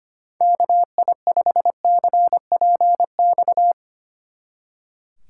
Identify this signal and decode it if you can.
Morse code, heard as a keyed tone, about 31 wpm, 700 Hz. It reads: KI5CPX
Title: Speed Racing - USA Call Signs 25wpm